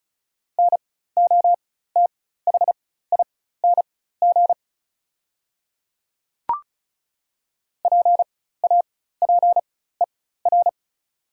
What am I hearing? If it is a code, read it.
Morse code, heard as a keyed tone, about 36 wpm, spaced as for 16 wpm, 700 Hz. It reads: NOTHING PAPER
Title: 4x Character Spacing - Top 500 Words - 35wpm